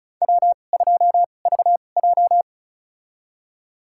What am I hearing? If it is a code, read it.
Morse code, heard as a keyed tone, about 36 wpm, spaced as for 25 wpm, 700 Hz. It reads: W2VJ
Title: USA Call Signs - 4 Characters - 2x Character Spacing - 35wpm